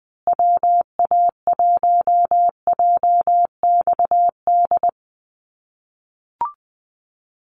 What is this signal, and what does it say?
Morse code, heard as a keyed tone, about 20 wpm, 700 Hz. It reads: WA1JXD
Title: USA Call Signs Repeated 2x 20wpm